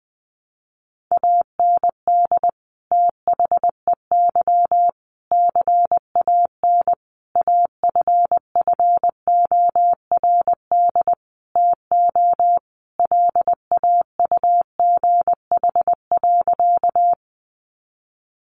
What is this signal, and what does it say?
Morse code, heard as a keyed tone, about 20 wpm, 700 Hz. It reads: AND THEY CAN AFFORD TO LAUGH.
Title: Sentences from Top 1100 Words 20wpm